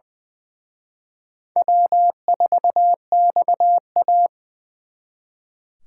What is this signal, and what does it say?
Morse code, heard as a keyed tone, about 23 wpm, 690 Hz. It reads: W4XA
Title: Speed Racing - USA 1x2 Call Signs 20wpm